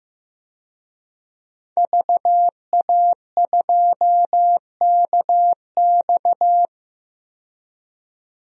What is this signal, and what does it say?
Morse code, heard as a keyed tone, about 15 wpm, 700 Hz. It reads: VA2KX
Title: DXCC-100 Call Signs - 5 Characters - 15wpm